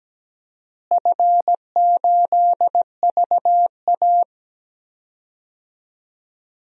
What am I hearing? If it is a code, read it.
Morse code, heard as a keyed tone, about 17 wpm, 700 Hz. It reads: F8VA
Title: DXCC-100 Call Signs - 4 Characters - Repeated 2x - 17wpm